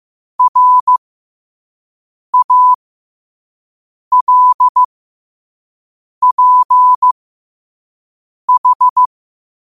Morse code, heard as a keyed tone, about 15 wpm, spaced as for 5 wpm, 1000 Hz. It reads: RALPH
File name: qotdfiles / 5wpm